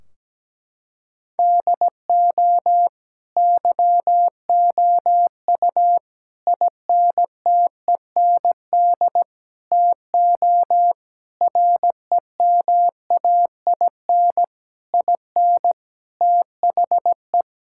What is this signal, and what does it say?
Morse code, heard as a keyed tone, about 17 wpm, 700 Hz. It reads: DO YOU INTEND TO REMAIN IN THE
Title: Sentences from Top 1600 Words 17wpm